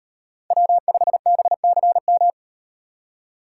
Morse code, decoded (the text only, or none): W5BCM